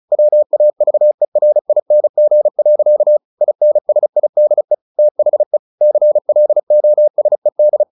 Morse code, heard as a keyed tone, about 34 wpm, 600 Hz. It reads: WAVERING. INSIDE THE CLOSED
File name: the old man and the sea 2 35wpm